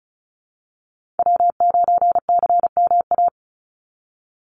Morse code, heard as a keyed tone, about 34 wpm, 700 Hz. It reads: W9CMA
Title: USA Call Signs 35wpm